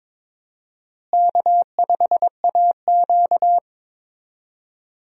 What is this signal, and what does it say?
Morse code, heard as a keyed tone, about 22 wpm, 700 Hz. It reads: K5AQ
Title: USA Call Signs - 4 Characters - Repeated 2x - 22wpm